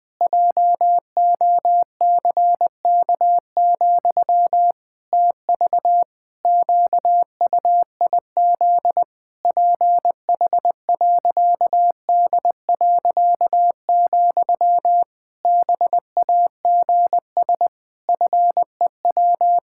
Morse code, heard as a keyed tone, about 20 wpm, 700 Hz. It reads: JOCK, TV QUIZ PH.D., BAGS FEW
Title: Pangram Challenge v2 - 20wpm